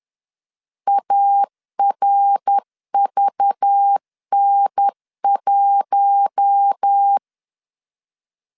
Morse code, heard as a keyed tone, about 10 wpm, 800 Hz. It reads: ARVN1